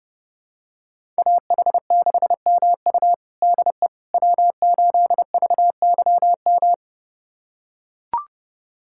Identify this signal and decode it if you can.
Morse code, heard as a keyed tone, about 30 wpm, 700 Hz. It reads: AH6MU DE W8VYM
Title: USA Callsign DE Callsign - 5 Characters - 30wpm